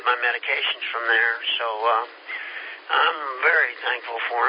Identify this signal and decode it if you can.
Emergency Alert System (SAME) tones, heard as harsh buzzing bursts, with speech. It is heard from 1.0 to 3.8 s.